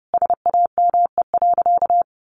Morse code, heard as a keyed tone, about 30 wpm, 700 Hz. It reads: SAME.
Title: Sentences from Top 300 Words 30wpm